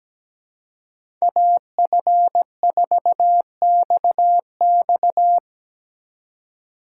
Morse code, heard as a keyed tone, about 18 wpm, 700 Hz. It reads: AF4XX
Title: Speed Racing - USA 2x2 Call Signs 17wpm